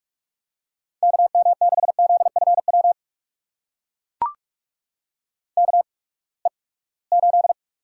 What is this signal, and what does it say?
Morse code, heard as a keyed tone, about 42 wpm, spaced as for 25 wpm, 700 Hz. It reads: KM6ZFW KE8
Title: USA Call Signs - 6 Characters - 8x Character Spacing - 45wpm